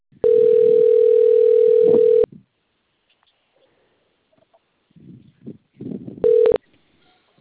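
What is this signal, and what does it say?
Ringback tone (North America), 440 + 480 Hz.
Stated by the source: Outgoing telephone call recording, Sao Carlos, Brazil.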